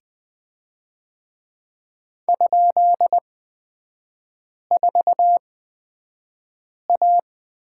Morse code, heard as a keyed tone, about 20 wpm, spaced as for 5 wpm, 700 Hz. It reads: ?4A
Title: 243 Lesson 39e - Letter z - Three Characters 20wpm